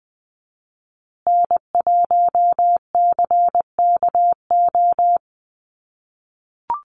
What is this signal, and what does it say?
Morse code, heard as a keyed tone, about 20 wpm, 700 Hz. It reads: N1CKO
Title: USA Call Signs Repeated 3x 20wpm